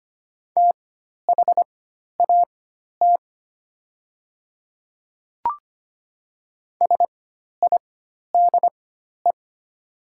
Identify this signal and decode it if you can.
Morse code, heard as a keyed tone, about 25 wpm, spaced as for 12 wpm, 700 Hz. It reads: THAT SIDE
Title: 4x Character Spacing - Top 100 Words - 25wpm